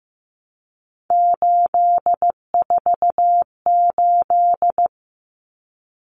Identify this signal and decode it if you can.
Morse code, heard as a keyed tone, about 15 wpm, 700 Hz. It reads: 848
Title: Numbers - 3-Digit 15wpm